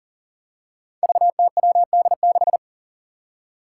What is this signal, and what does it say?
Morse code, heard as a keyed tone, about 40 wpm, 700 Hz. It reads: VTWD6
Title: Five Challenging Letter-Number Combinations 40wpm